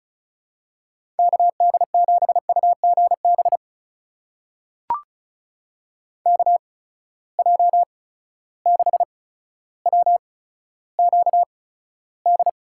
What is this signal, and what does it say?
Morse code, heard as a keyed tone, about 36 wpm, spaced as for 15 wpm, 700 Hz. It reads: KD7UGB KJ6WQD
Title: USA Call Signs - 6 Characters - 8x Character Spacing - 35wpm